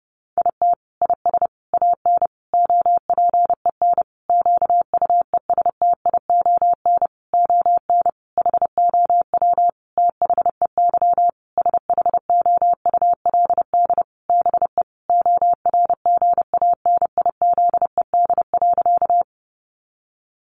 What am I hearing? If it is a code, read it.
Morse code, heard as a keyed tone, about 30 wpm, 700 Hz. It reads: IT IS AN OPEN QUESTION ON HOW THEY SHOULD BE ORGANIZED.
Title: Sentences from Top 1300 Words 30wpm